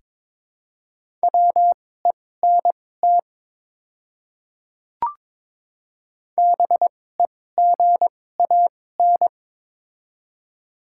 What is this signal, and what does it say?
Morse code, heard as a keyed tone, about 22 wpm, spaced as for 16 wpm, 700 Hz. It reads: WENT BEGAN
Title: 2x Character Spacing - Top 300 Words - 22wpm